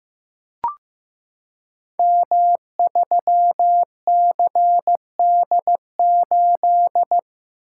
Morse code, heard as a keyed tone, about 15 wpm, 700 Hz. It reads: M3CD8
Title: Code Groups - 5 Letter:Numbers - 15wpm